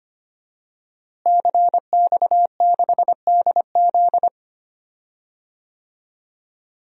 Morse code, heard as a keyed tone, about 25 wpm, 700 Hz. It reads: CX6DZ
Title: DXCC-100 Call Signs - 5 Characters - Repeated 3x - 25wpm